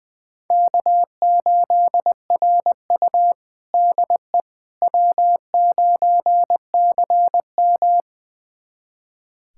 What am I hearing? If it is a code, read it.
Morse code, heard as a keyed tone, about 20 wpm, 700 Hz. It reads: K8RU DE W9CM
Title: USA Callsign DE Callsign - 4 Characters - 20wpm